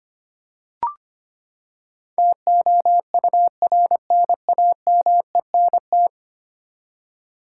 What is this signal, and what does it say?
Morse code, heard as a keyed tone, about 25 wpm, 700 Hz. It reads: TOURNAMENT
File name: Ten Letter Words - 25wpm